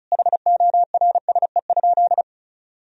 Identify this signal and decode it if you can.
Morse code, heard as a keyed tone, about 36 wpm, 700 Hz. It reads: HORSE?
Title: Sentences from Top 1000 Words - Review Part 1 - 35wpm